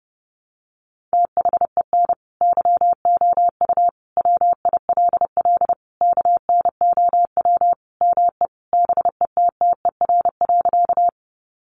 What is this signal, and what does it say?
Morse code, heard as a keyed tone, about 30 wpm, 700 Hz. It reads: THEN YOU WILL KNOW ME BETTER.
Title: Sentences from Top 400 Words 30wpm